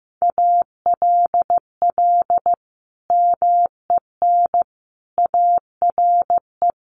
Morse code, heard as a keyed tone, about 15 wpm, 700 Hz. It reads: ALL MEN ARE
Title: Sentences from Top 1500 Words 15wpm